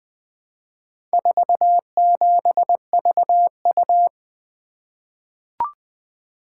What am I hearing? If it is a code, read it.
Morse code, heard as a keyed tone, about 21 wpm, 700 Hz. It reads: 47VU
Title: Four Letter-Number Combinations - 20wpm